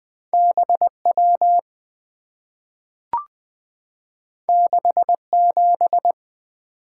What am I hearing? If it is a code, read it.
Morse code, heard as a keyed tone, about 20 wpm, 700 Hz. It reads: BW 67
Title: Two Letter-Number Combinations - 20wpm